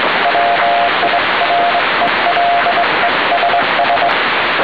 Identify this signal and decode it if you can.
Morse code, heard as a keyed tone, about 22 wpm, 700 Hz, with noise. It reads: WIRELESS